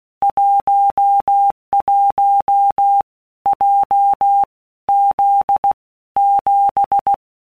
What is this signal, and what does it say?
Morse code, heard as a keyed tone, about 16 wpm, 800 Hz. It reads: 11 J Z 7
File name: Basic Letter Groups By Blocks / Groups@15 / JZ17@15